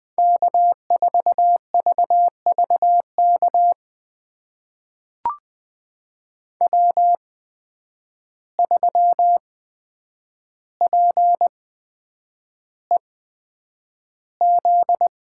Morse code, heard as a keyed tone, about 20 wpm, spaced as for 9 wpm, 700 Hz. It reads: K4VVK W3PEZ
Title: USA Call Signs - 5 Characters - 8x Character Spacing - 20wpm